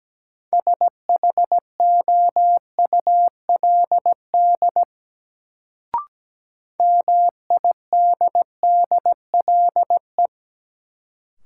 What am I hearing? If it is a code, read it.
Morse code, heard as a keyed tone, about 17 wpm, 700 Hz. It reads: SHOULD MIDDLE
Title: Sets of 2 Words - New Word Drill - 17wpm